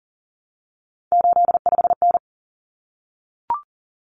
Morse code, heard as a keyed tone, about 40 wpm, 700 Hz. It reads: 85N
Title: Common Cut Numbers - 3-Digit 40wpm